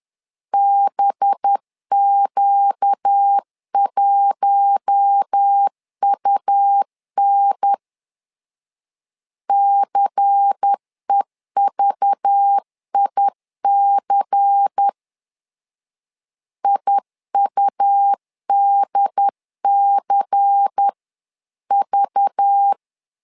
Morse code, heard as a keyed tone, about 10 wpm, 800 Hz. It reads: BQ1UN CEVIC IUDCV
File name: LEZIONI / 5 LBVC / 5 LBVC 10wpm 2349